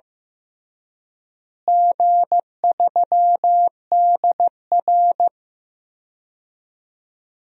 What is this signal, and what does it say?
Morse code, heard as a keyed tone, about 15 wpm, 700 Hz. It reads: G3DR
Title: DXCC-100 Call Signs - 4 Characters - Repeated 3x - 15wpm